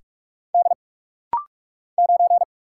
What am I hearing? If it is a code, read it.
Morse code, heard as a keyed tone, about 45 wpm, 700 Hz. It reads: D 9
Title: Single Letter-Number Alt - Rapid-Fire 45wpm